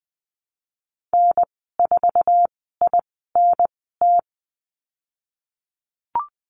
Morse code, heard as a keyed tone, about 20 wpm, spaced as for 14 wpm, 700 Hz. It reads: N4INT
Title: 088 Lesson 15g - Number 5 - Call Signs 20wpm